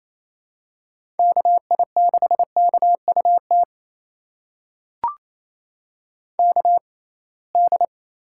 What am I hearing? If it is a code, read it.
Morse code, heard as a keyed tone, about 28 wpm, spaced as for 21 wpm, 700 Hz. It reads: KI6KUT KD
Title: USA Call Signs - 6 Characters - 6x Character Spacing - 28wpm